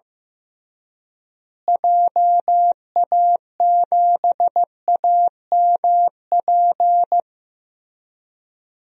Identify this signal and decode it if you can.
Morse code, heard as a keyed tone, about 15 wpm, 700 Hz. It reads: JA7AMP
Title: DXCC-100 Call Signs - 6 Characters - Repeated 3x - 15wpm